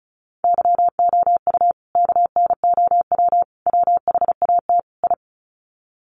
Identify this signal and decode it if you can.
Morse code, heard as a keyed tone, about 34 wpm, 700 Hz. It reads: YOU KNOW WHAT I
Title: Sets of 4 Words 35wpm